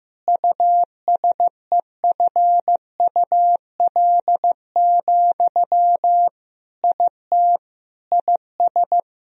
Morse code, heard as a keyed tone, about 15 wpm, 700 Hz. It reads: USEFUL, IT IS
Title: Sentences from Top 1000 Words - Review Part 4 - 15wpm